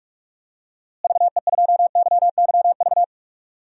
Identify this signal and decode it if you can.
Morse code, heard as a keyed tone, about 41 wpm, 700 Hz. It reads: VE2YYV
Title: DXCC-100 Call Signs - 6 Characters - Repeated 2x - 45wpm